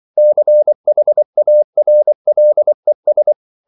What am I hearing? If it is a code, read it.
Morse code, heard as a keyed tone, about 24 wpm, 600 Hz. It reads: CHARLES
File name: CW / Practice / EnglishQuotations0424